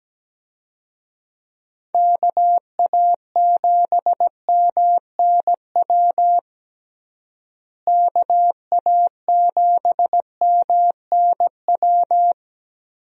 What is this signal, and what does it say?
Morse code, heard as a keyed tone, about 17 wpm, 700 Hz. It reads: KA7MNW KA7MNW
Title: USA Call Signs - 6 Characters - Repeated 3x - 17wpm